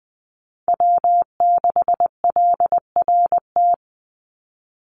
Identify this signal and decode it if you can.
Morse code, heard as a keyed tone, about 20 wpm, 700 Hz. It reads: W6LRT
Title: USA Call Signs 20wpm